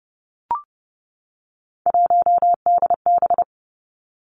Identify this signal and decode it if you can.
Morse code, heard as a keyed tone, about 30 wpm, 700 Hz. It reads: 1DB
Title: Three Challenging Letter-Number Combinations 30wpm